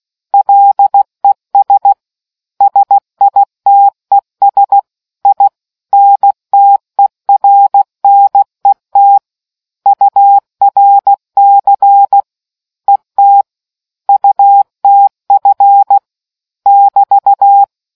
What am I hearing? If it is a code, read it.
Morse code, heard as a keyed tone, about 16 wpm, 800 Hz. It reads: LES SITES INTERNET URC ET UTF <BT>